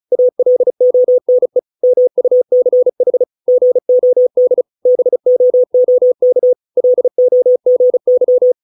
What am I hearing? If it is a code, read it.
Morse code, heard as a keyed tone, about 34 wpm, 500 Hz. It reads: ALONE MUCH GOD BOOK LOGY